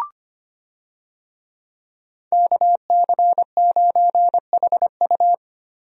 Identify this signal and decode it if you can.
Morse code, heard as a keyed tone, about 25 wpm, 700 Hz. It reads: KC9HU
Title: USA Call Signs - 5 Characters - Repeated 3x - 25wpm